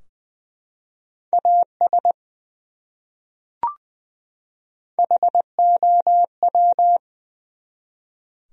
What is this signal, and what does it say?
Morse code, heard as a keyed tone, about 20 wpm, 700 Hz. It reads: AS HOW
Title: Top 100 Words 20wpm